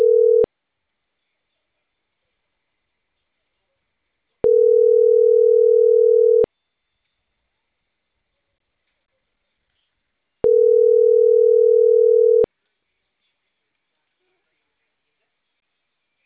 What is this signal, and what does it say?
Ringback tone (North America), 440 + 480 Hz.